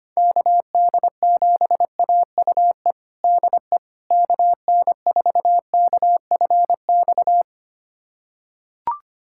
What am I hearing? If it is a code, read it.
Morse code, heard as a keyed tone, about 25 wpm, 700 Hz. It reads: KD7AUE DE KN4KFX
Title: USA Callsign DE Callsign - 6 Characters - 25wpm